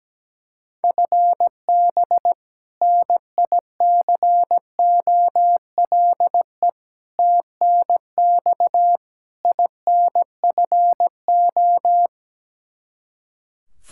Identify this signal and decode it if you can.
Morse code, heard as a keyed tone, about 17 wpm, 700 Hz. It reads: FB NICOLE TNX INFO
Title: Practice QSO - Exchange 3 - More Info - 17wpm